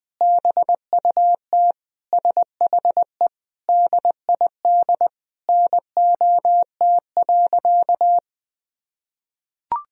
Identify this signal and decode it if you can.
Morse code, heard as a keyed tone, about 21 wpm, 700 Hz. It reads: BUT SHE DID NOT.
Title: Sentences from Top 1000 Words - Review Part 1 - 20wpm